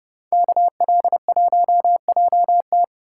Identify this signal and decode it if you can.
Morse code, heard as a keyed tone, about 30 wpm, 700 Hz. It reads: KL1JT